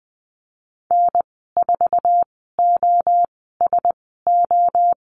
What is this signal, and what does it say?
Morse code, heard as a keyed tone, about 20 wpm, spaced as for 15 wpm, 700 Hz. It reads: N4OSO
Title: 069 Lesson 12g - Letter d - Call Signs 20wpm